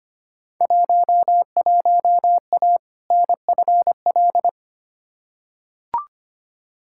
Morse code, heard as a keyed tone, about 25 wpm, 700 Hz. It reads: 11A NFL
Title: Field Day Exchanges Encore - 25wpm